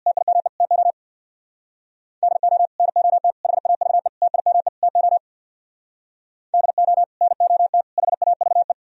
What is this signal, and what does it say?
Morse code, heard as a keyed tone, about 63 wpm, 700 Hz. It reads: TIME TO DO NOT HAVE TIME TO DO NOT HAVE